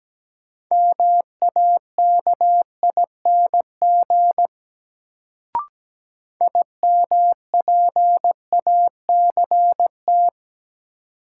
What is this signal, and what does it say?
Morse code, heard as a keyed tone, about 17 wpm, 700 Hz. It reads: MAKING IMPACT
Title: Sentences from Top 1000 Words - Review Part 6 - New Word Drill - 17wpm